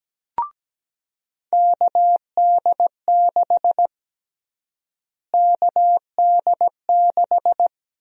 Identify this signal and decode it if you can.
Morse code, heard as a keyed tone, about 17 wpm, 700 Hz. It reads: KD6 KD6
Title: USA Call Sign Prefixes - Repeated 2x - 17wpm